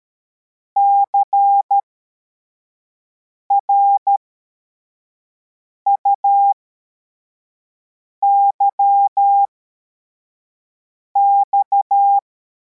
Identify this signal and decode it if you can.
Morse code, heard as a keyed tone, about 13 wpm, spaced as for 4 wpm, 800 Hz. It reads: CRUYX
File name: formation / cw / Text85